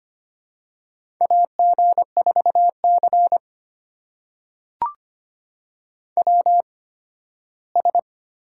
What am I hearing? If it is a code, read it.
Morse code, heard as a keyed tone, about 25 wpm, spaced as for 15 wpm, 700 Hz. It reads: AG4C WS
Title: USA Call Signs - 4 Characters - 8x Character Spacing - 25wpm